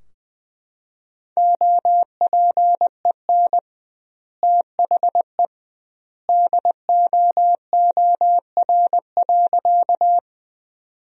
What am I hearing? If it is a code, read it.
Morse code, heard as a keyed tone, about 21 wpm, 700 Hz. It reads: OPEN THE DOOR.
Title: 2x Word-Spacing - Sentences from Top 300 Words - 20wpm